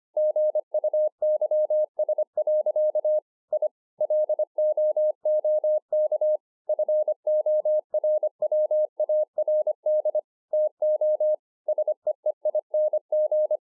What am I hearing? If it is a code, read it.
Morse code, heard as a keyed tone, about 25 wpm, 610 Hz. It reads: GUYS. I LOOK FORWARD TO SEEING